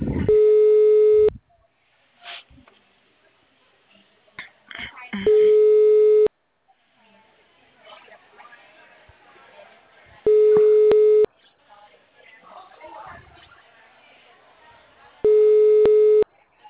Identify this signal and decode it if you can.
Ringback tone (Europe, 425 Hz), with speech.
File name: out- -2006- - - .55